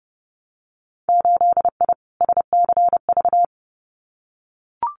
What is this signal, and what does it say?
Morse code, heard as a keyed tone, about 30 wpm, 700 Hz. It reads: 8I SCV
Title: Winter Field Day Exchanges Encore 30wpm